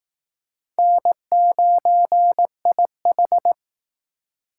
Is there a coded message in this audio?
Yes: Morse code from 0.8 to 3.5 s, heard as a keyed tone, about 19 wpm, 700 Hz. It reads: N9IH